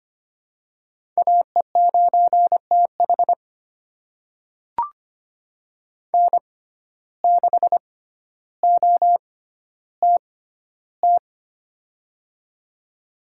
Morse code, heard as a keyed tone, about 25 wpm, spaced as for 13 wpm, 700 Hz. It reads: AE9TH N6OTT
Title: USA Call Signs - 5 Characters - 6x Character Spacing - 25wpm